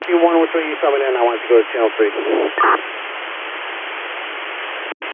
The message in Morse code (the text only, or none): none